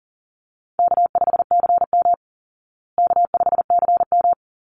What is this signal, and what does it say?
Morse code, heard as a keyed tone, about 40 wpm, 700 Hz. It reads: K5CM K5CM
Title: Easy Call Signs Repeated 2x - 40wpm